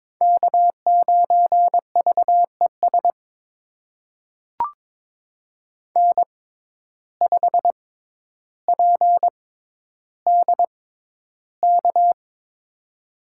Morse code, heard as a keyed tone, about 22 wpm, spaced as for 11 wpm, 700 Hz. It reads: K9VES N5PDK
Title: USA Call Signs - 5 Characters - 6x Character Spacing - 22wpm